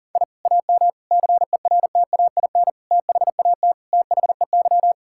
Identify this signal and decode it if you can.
Morse code, heard as a keyed tone, about 40 wpm, 700 Hz. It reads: I AM CERTAIN THAT THEY
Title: Sentences from Top 1000 Words - Review Part 6 - 40wpm